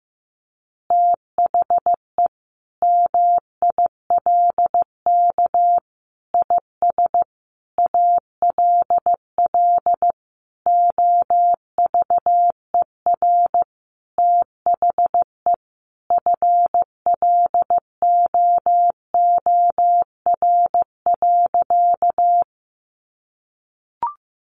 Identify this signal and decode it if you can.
Morse code, heard as a keyed tone, about 14 wpm, 700 Hz. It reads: THE MILK IS ALL OVER THE FLOOR.
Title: Additional Easy Phrases 15wpm